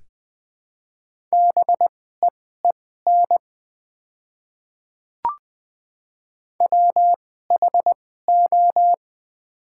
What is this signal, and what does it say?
Morse code, heard as a keyed tone, about 20 wpm, spaced as for 14 wpm, 690 Hz. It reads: BEEN WHO
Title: 2x Character Spacing - Top 500 Words - 20wpm